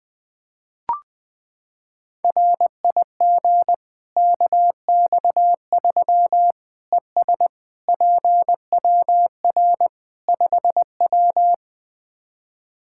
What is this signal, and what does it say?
Morse code, heard as a keyed tone, about 20 wpm, 700 Hz. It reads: RIG KX3 ES PWR 5W
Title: Practice QSO - Exchange 2 - Info - 20wpm